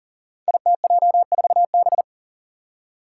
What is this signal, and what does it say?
Morse code, heard as a keyed tone, about 40 wpm, 700 Hz. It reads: ITJ4B
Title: Five Challenging Letter-Number Combinations 40wpm